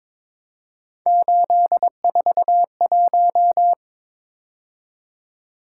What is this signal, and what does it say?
Morse code, heard as a keyed tone, about 22 wpm, 700 Hz. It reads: 841
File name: practice / 22wpm / Three Letter-Number Combinations - 22wpm